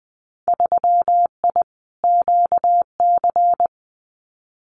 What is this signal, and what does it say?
Morse code, heard as a keyed tone, about 19 wpm, 700 Hz. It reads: 3I QC
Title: Winter Field Day Exchanges Encore 20wpm